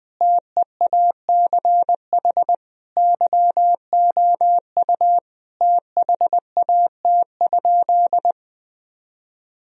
Morse code, heard as a keyed tone, about 20 wpm, 700 Hz. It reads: TEACH YOU THAT?
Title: Sentences from Top 1000 Words - Review Part 2 - 20wpm